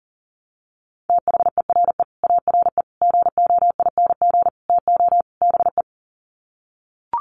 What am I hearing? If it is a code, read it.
Morse code, heard as a keyed tone, about 40 wpm, 700 Hz. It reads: THERE ARE GOING TO BE
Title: Sets of 5 Words 40wpm